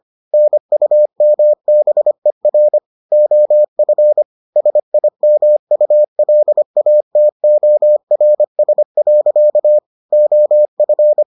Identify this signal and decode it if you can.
Morse code, heard as a keyed tone, about 25 wpm, 600 Hz. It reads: NUMBER OF SIMULATORS. OF